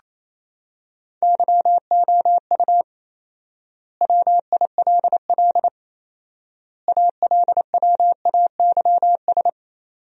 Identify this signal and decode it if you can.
Morse code, heard as a keyed tone, about 28 wpm, 700 Hz. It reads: YOU WILL ALWAYS
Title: 4x Word-Spacing - Sentences from Top 400 Words - 28wpm